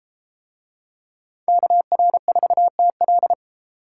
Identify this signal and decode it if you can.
Morse code, heard as a keyed tone, about 32 wpm, 700 Hz. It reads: KR4TL